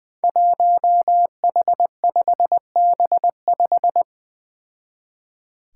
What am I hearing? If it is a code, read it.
Morse code, heard as a keyed tone, about 20 wpm, 700 Hz. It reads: 1H5B5
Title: Five Challenging Letter-Number Combinations 20wpm